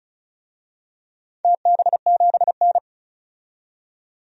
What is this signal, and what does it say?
Morse code, heard as a keyed tone, about 36 wpm, 700 Hz. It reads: TB7N
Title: DXCC-100 Call Signs - 4 Characters - 35wpm